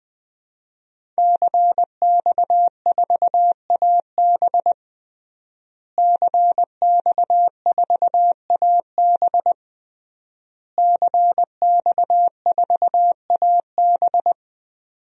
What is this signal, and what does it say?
Morse code, heard as a keyed tone, about 20 wpm, 700 Hz. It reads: CX4AB CX4AB CX4AB
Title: DXCC-100 Call Signs - 5 Characters - Repeated 3x - 20wpm